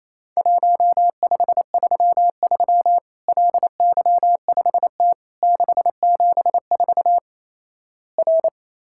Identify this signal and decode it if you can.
Morse code, heard as a keyed tone, about 28 wpm, 700 Hz. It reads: 1533 LY5T 674 R
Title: Worked All Europe DX Contest - QTC Exchange - 28wpm